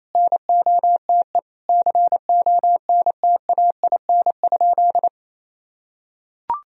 Morse code, heard as a keyed tone, about 28 wpm, 700 Hz. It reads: NOTE CONTAIN?
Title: Sentences from Top 1000 Words - Review Part 2 - 28wpm